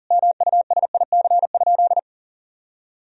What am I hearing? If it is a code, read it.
Morse code, heard as a keyed tone, about 40 wpm, 700 Hz. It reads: MUSIC?
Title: Sentences from Top 300 Words and Abbreviations 40wpm